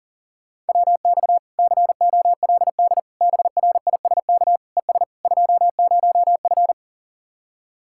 Morse code, heard as a keyed tone, about 52 wpm, 700 Hz. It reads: WX COLD BRISK ES 20F